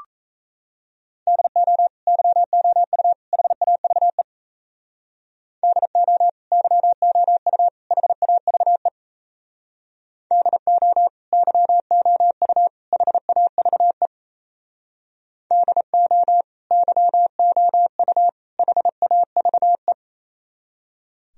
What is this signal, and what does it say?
Morse code, heard as a keyed tone, about 37 wpm, 700 Hz. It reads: DO YOU HAVE DO YOU HAVE DO YOU HAVE DO YOU HAVE